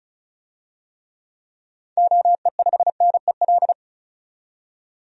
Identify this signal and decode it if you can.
Morse code, heard as a keyed tone, about 34 wpm, 700 Hz. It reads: OE5NEL
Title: DXCC-100 Call Signs - 6 Characters - 35wpm